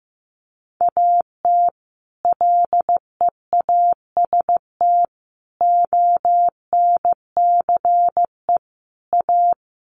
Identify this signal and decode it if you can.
Morse code, heard as a keyed tone, about 15 wpm, 700 Hz. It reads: AT LEAST ONCE A
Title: Sets of 5 Words Encore2 15wpm